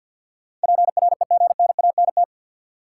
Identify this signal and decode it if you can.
Morse code, heard as a keyed tone, about 53 wpm, 700 Hz. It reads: PREGNANT
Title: Sentences from Top 1000 Words - Review Part 6 - New Word Drill - 50wpm